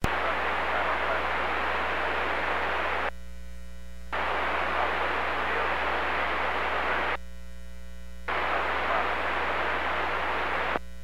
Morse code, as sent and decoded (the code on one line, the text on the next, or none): none